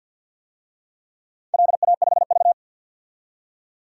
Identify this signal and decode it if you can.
Morse code, heard as a keyed tone, about 49 wpm, 700 Hz. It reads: LA5V
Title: DXCC-100 Call Signs - 4 Characters - 50wpm